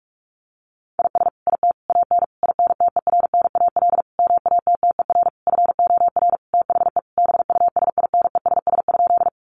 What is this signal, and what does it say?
Morse code, heard as a keyed tone, about 44 wpm, 700 Hz. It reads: IS IT AN INTERNAL MATTER FOR THE BUSINESS?
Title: Sentences from Top 1400 Words 45wpm